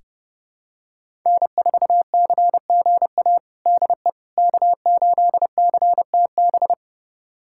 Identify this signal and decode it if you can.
Morse code, heard as a keyed tone, about 30 wpm, 700 Hz. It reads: N4CGA DE K8CTB
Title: USA Callsign DE Callsign - 5 Characters - 30wpm